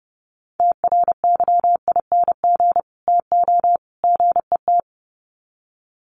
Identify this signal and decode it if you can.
Morse code, heard as a keyed tone, about 30 wpm, 700 Hz. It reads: TRYING TO GET
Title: Sets of 3 Words 30wpm